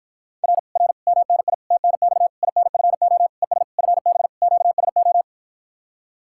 Morse code, heard as a keyed tone, about 53 wpm, 700 Hz. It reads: R R MNI TNX INFO ES FB QSO